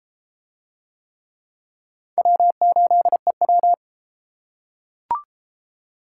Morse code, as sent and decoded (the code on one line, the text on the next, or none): .-- ---.. .--
W8EW